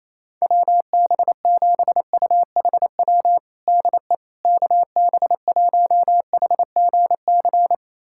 Morse code, heard as a keyed tone, about 27 wpm, 700 Hz. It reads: WB7UHW DE KB1HGC
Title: USA Callsign DE Callsign 28wpm